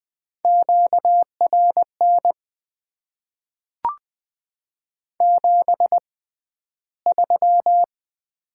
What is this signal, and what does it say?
Morse code, heard as a keyed tone, about 20 wpm, spaced as for 12 wpm, 700 Hz. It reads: QRN 73
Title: Top 90 QSO Elements - 6x Character Spacing - 20wpm